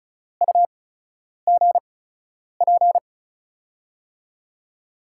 Morse code, heard as a keyed tone, about 36 wpm, spaced as for 10 wpm, 700 Hz. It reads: UGP
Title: USA Call Signs - 6 Characters - 8x Character Spacing - 35wpm